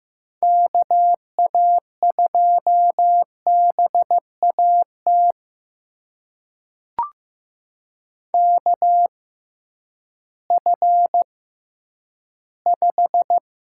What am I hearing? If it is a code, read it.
Morse code, heard as a keyed tone, about 15 wpm, spaced as for 10 wpm, 700 Hz. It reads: KA2BAT KF5
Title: USA Call Signs - 6 Characters - 6x Character Spacing - 15wpm